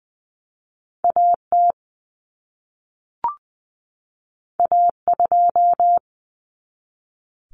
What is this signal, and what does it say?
Morse code, heard as a keyed tone, about 20 wpm, 700 Hz. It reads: AT A2
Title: Common Cut Numbers - 2-Digit 20wpm